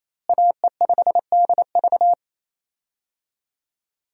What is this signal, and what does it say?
Morse code, heard as a keyed tone, about 27 wpm, 700 Hz. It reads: AE5DV
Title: USA Call Signs - 5 Characters - Repeated 2x - 28wpm